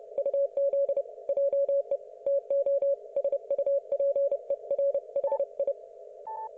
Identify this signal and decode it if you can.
Morse code, heard as a keyed tone, about 30 wpm, 560 Hz. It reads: UZ JE TO SUPER HI
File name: Archiv / CUC179 / ok1fhi cul179